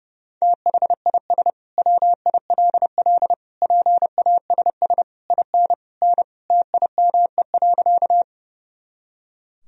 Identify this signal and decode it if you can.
Morse code, heard as a keyed tone, about 30 wpm, 700 Hz. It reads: THIS WILL PASS IN N TIME.
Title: Sentences from Top 400 Words and Abbreviations 30wpm